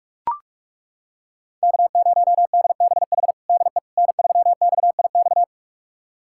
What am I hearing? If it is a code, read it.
Morse code, heard as a keyed tone, about 41 wpm, 700 Hz. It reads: K0DBH DE N3XIX
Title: USA Callsign DE Callsign - 5 Characters - 45wpm